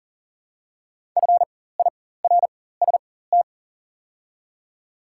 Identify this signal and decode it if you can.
Morse code, heard as a keyed tone, about 40 wpm, spaced as for 19 wpm, 700 Hz. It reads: FIRST
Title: 4x Character Spacing - Top 100 Words - 40wpm